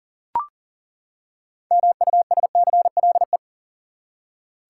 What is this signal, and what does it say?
Morse code, heard as a keyed tone, about 40 wpm, 700 Hz. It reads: MUSCLE
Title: Six Letter Words - 40wpm